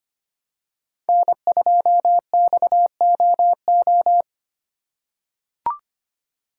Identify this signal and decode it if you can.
Morse code, heard as a keyed tone, about 25 wpm, 700 Hz. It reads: N2XOO
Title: USA Call Signs - 5 Characters - 3x Character Spacing - 25wpm